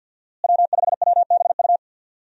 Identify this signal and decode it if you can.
Morse code, heard as a keyed tone, about 47 wpm, 700 Hz. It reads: W5WBU
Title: USA Call Signs - 5 Characters - Repeated 2x - 50wpm